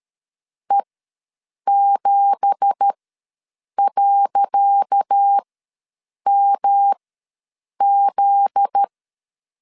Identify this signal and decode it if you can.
Morse code, heard as a keyed tone, about 13 wpm, spaced as for 7 wpm, 800 Hz. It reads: E7.MZ
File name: LEZIONI / 10 SIGNS / 10C DOT ws 12wpm 3548